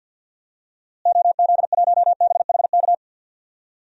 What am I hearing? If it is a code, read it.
Morse code, heard as a keyed tone, about 52 wpm, 700 Hz. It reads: OZ1BHK
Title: DXCC-100 Call Signs - 6 Characters - Repeated 3x - 50wpm